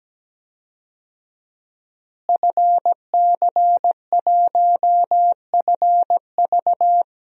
Morse code, heard as a keyed tone, about 17 wpm, 700 Hz. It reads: FC1FV
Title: DXCC-100 Call Signs - 5 Characters - Repeated 3x - 17wpm